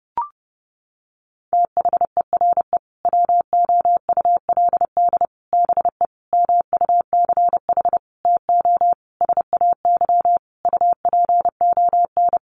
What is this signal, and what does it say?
Morse code, heard as a keyed tone, about 30 wpm, 700 Hz. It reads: THERE WOULD BE MUCH TO SAY UPON
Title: Sentences from Top 1200 Words 30wpm